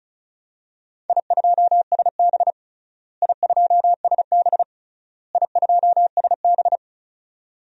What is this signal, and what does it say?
Morse code, heard as a keyed tone, about 36 wpm, 700 Hz. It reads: I2SB I2SB I2SB
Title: DXCC-100 Call Signs - 4 Characters - Repeated 3x - 35wpm